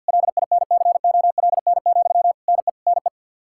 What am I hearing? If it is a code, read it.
Morse code, heard as a keyed tone, about 51 wpm, 700 Hz. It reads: LINCOLN, NE NE
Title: Practice QSO - Exchange 1 - Beginning - 50wpm